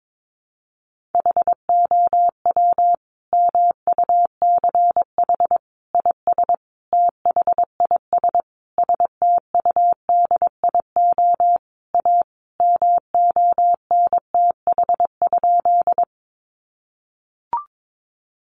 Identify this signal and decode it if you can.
Morse code, heard as a keyed tone, about 22 wpm, 700 Hz. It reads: HOW MUCH IS THIS STUDIO A MONTH?
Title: Sentences from Top 1200 Words 22wpm